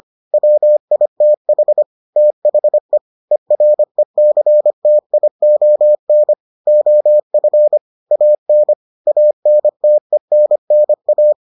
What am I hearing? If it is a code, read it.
Morse code, heard as a keyed tone, about 25 wpm, 600 Hz. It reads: WITH THE ERECTION OF AN ANTENNA